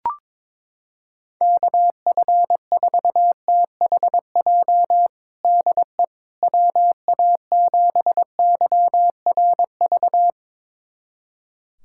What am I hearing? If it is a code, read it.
Morse code, heard as a keyed tone, about 22 wpm, 700 Hz. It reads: KF4THJ DE WA7YRV
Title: USA Callsign DE Callsign 22wpm